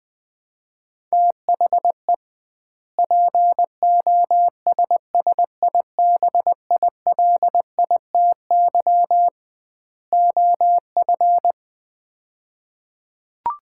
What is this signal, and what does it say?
Morse code, heard as a keyed tone, about 20 wpm, 700 Hz. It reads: THE POSSIBILITY OF
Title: 2x Word-Spacing - Sets of 3 Words - 20wpm